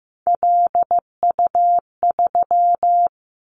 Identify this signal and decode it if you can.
Morse code, heard as a keyed tone, about 15 wpm, 700 Hz. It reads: LU3
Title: Callsign Prefixes for Top 100 DXCC - 15wpm